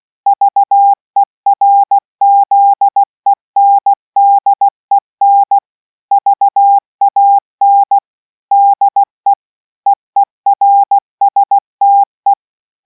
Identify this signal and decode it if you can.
Morse code, heard as a keyed tone, about 16 wpm, 800 Hz. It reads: VERZENDEN VAN DE EERSTE